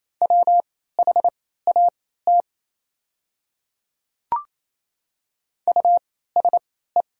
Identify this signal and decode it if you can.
Morse code, heard as a keyed tone, about 27 wpm, spaced as for 16 wpm, 690 Hz. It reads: WHAT USE
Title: 3x Character Spacing - Top 100 Words - 28wpm